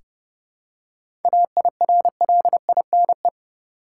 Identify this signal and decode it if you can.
Morse code, heard as a keyed tone, about 30 wpm, 700 Hz. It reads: AIRLINE
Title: Seven Letter Words - 30wpm